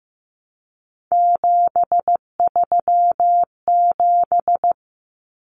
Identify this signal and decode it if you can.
Morse code, heard as a keyed tone, about 15 wpm, 700 Hz. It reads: 737
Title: Numbers - 3-Digit 15wpm